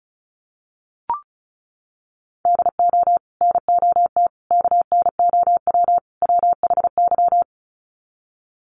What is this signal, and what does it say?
Morse code, heard as a keyed tone, about 34 wpm, 700 Hz. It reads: DO NOT KNOW WHY
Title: Sets of 4 Words 35wpm